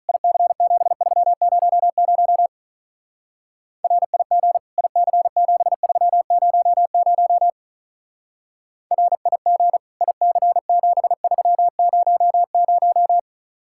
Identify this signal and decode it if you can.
Morse code, heard as a keyed tone, about 42 wpm, 700 Hz. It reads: IC7300 RIG IC7300 RIG IC7300
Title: Speed-Racing Common QSO Phrases - 35wpm